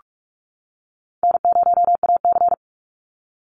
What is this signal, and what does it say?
Morse code, heard as a keyed tone, about 46 wpm, 700 Hz. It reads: N0AC
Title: Easy Call Signs - 45wpm